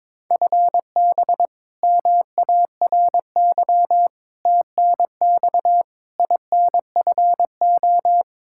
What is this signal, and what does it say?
Morse code, heard as a keyed tone, about 22 wpm, 700 Hz. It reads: FB MARY TNX INFO
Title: Practice QSO - Exchange 3 - More Info - 22wpm